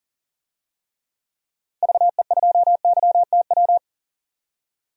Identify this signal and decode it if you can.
Morse code, heard as a keyed tone, about 37 wpm, 700 Hz. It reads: VE2YTW
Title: DXCC-100 Call Signs - 6 Characters - 40wpm